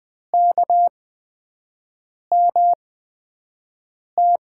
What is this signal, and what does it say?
Morse code, heard as a keyed tone, about 20 wpm, spaced as for 5 wpm, 700 Hz. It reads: KMT
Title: USA Call Signs - 5 Characters - 8x Character Spacing - 20wpm